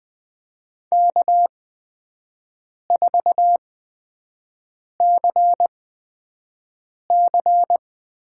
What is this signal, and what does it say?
Morse code, heard as a keyed tone, about 20 wpm, spaced as for 5 wpm, 700 Hz. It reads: K4CC
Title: USA Call Signs - 5 Characters - 8x Character Spacing - 20wpm